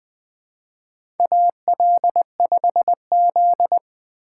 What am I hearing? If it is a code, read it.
Morse code, heard as a keyed tone, about 20 wpm, 700 Hz. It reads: AL5Z
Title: USA Call Signs - 4 Characters - 2x Character Spacing - 20wpm